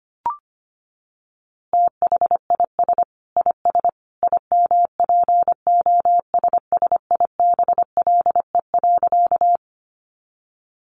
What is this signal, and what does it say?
Morse code, heard as a keyed tone, about 25 wpm, 700 Hz. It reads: THIS IS IMPOSSIBLE.
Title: A Sound Of Thunder 25wpm